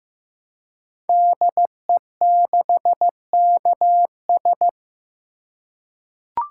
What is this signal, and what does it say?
Morse code, heard as a keyed tone, about 15 wpm, 700 Hz. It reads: DE6KS
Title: Code Groups - 5 Letter:Numbers - 15wpm